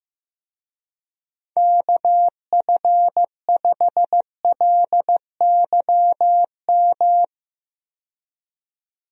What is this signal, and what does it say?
Morse code, heard as a keyed tone, about 15 wpm, 700 Hz. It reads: KF5LYM
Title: USA Call Signs - 6 Characters - 15wpm